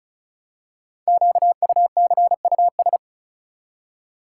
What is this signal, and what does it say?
Morse code, heard as a keyed tone, about 36 wpm, 700 Hz. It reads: QUCUS
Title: Code Groups - 5 Letters - 35wpm